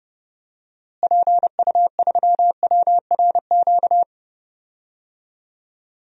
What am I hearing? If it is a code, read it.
Morse code, heard as a keyed tone, about 30 wpm, 700 Hz. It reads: PU3WRQ
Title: DXCC-100 Call Signs - 6 Characters - Repeated 2x - 30wpm